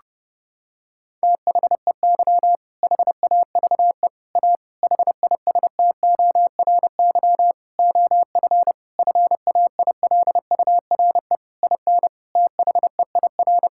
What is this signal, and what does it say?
Morse code, heard as a keyed tone, about 30 wpm, 700 Hz. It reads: THEY HAVE A HISTORY OF FAILURE IN THEIR
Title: Sentences from Top 1000 Words - Review Part 6 - 30wpm